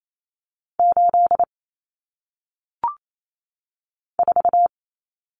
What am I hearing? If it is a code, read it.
Morse code, heard as a keyed tone, about 28 wpm, 700 Hz. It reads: 8 4
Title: Numbers - 1-Digit 28wpm